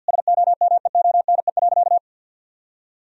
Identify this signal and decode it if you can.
Morse code, heard as a keyed tone, about 50 wpm, 700 Hz. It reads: SOMEONE.
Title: Sentences from Top 1000 Words - Review Part 4 - 50wpm